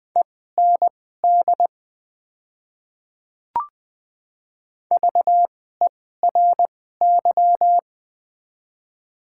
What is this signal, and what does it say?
Morse code, heard as a keyed tone, about 20 wpm, spaced as for 15 wpm, 700 Hz. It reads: END VERY
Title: 2x Character Spacing - Top 300 Words - 20wpm